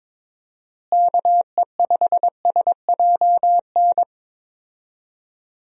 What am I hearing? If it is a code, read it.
Morse code, heard as a keyed tone, about 22 wpm, 700 Hz. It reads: KE5SJN
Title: USA Call Signs - 6 Characters - Repeated 3x - 22wpm